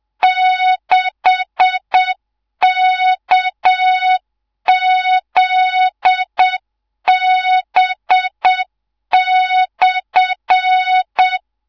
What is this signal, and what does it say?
Morse code, heard as a keyed tone, about 7 wpm, 3700 Hz, with noise. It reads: 6KZB/
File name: Morze-HA2MN / hangp / 2 03 GYAKORLAT